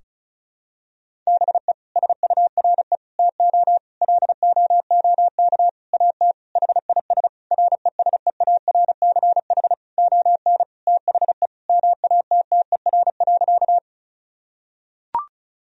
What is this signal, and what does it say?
Morse code, heard as a keyed tone, about 36 wpm, 700 Hz. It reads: BE SURE TO LOOK AT HIS RESEARCH ON THE MATTER.
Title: Sentences from Top 1000 Words - Review Part 4 - 35wpm